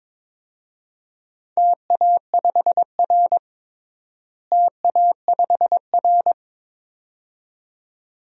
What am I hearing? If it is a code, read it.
Morse code, heard as a keyed tone, about 22 wpm, 700 Hz. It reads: TA5R TA5R
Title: DXCC-100 Call Signs - 4 Characters - Repeated 2x - 22wpm